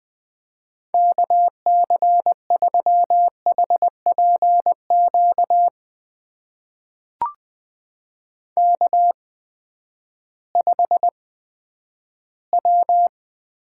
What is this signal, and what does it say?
Morse code, heard as a keyed tone, about 20 wpm, spaced as for 11 wpm, 700 Hz. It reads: KC3HPQ K5W
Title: USA Call Signs - 8x Character Spacing - 20wpm